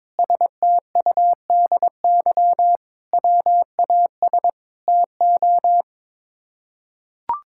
Morse code, heard as a keyed tone, about 22 wpm, 700 Hz. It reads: STUDY WAS TO
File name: practice / 22wpm / Sets of 7 Words - 22wpm